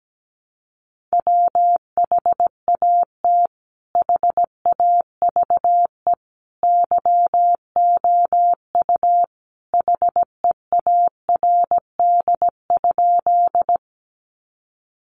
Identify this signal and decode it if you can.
Morse code, heard as a keyed tone, about 17 wpm, 700 Hz. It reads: WHAT HAVE YOU HEARD?
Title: Sentences from Top 400 Words 17wpm